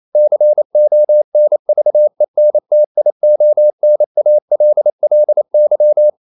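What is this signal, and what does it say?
Morse code, heard as a keyed tone, about 28 wpm, 600 Hz. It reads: CONVENTIONALLY
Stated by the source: Morse code practice at 28 wpm, text of Ken Follett's novel Never in English, from CW med Gnister.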